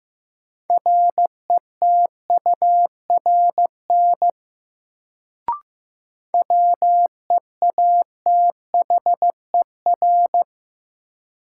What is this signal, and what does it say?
Morse code, heard as a keyed tone, about 15 wpm, 700 Hz. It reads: RETURN WEATHER
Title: Sentences from Top 800 Words - New Word Drill - 15wpm